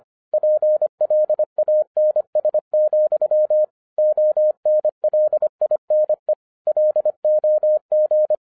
Morse code, heard as a keyed tone, about 25 wpm, 610 Hz. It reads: PLANS, ONLINE LOG